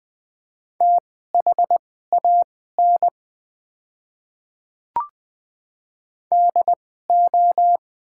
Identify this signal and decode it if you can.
Morse code, heard as a keyed tone, about 20 wpm, spaced as for 14 wpm, 700 Hz. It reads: THAN DO
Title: 2x Character Spacing - Top 100 Words - 20wpm